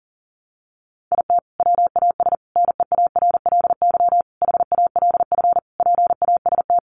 Morse code, heard as a keyed tone, about 40 wpm, 700 Hz. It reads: IT WAS NEARLY HALF PAST
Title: Sentences from Top 1300 Words 40wpm